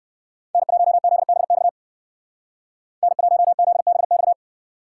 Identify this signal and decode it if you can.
Morse code, heard as a keyed tone, about 57 wpm, 700 Hz. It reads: N1ZBX N1ZBX